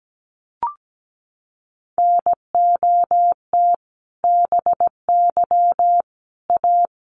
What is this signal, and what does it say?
Morse code, heard as a keyed tone, about 17 wpm, 700 Hz. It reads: NOT BY A
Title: Sentences from Top 900 Words 17wpm